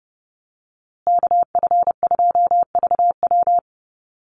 Morse code, heard as a keyed tone, about 30 wpm, 700 Hz. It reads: KF2VW
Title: USA Call Signs Repeated 3x 30wpm